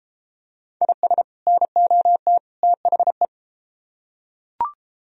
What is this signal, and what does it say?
Morse code, heard as a keyed tone, about 30 wpm, 700 Hz. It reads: IS NOT THE